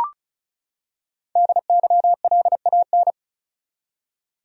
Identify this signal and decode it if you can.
Morse code, heard as a keyed tone, about 36 wpm, 700 Hz. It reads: DYLAN
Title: USA Common Names - Max 5 Letters - 35wpm